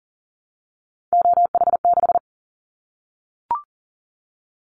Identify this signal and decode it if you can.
Morse code, heard as a keyed tone, about 38 wpm, 700 Hz. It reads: OH6
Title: Callsign Prefixes for Top 50 DXCC - 40wpm